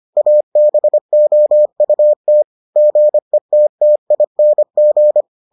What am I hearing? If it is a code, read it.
Morse code, heard as a keyed tone, about 25 wpm, 600 Hz. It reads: ABOUT GETTING